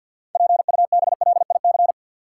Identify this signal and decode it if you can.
Morse code, heard as a keyed tone, about 47 wpm, 700 Hz. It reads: PUBLIC
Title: Sets of 2 Words - New Word Drill - 50wpm